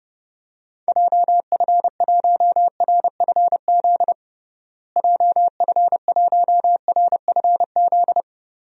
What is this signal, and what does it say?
Morse code, heard as a keyed tone, about 32 wpm, 700 Hz. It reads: JF1RFZ JF1RFZ
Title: DXCC-100 Call Signs - 6 Characters - Repeated 2x - 30wpm